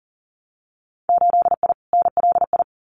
Morse code, heard as a keyed tone, about 40 wpm, 700 Hz. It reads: 8I NLI
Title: Winter Field Day Exchanges Encore 40wpm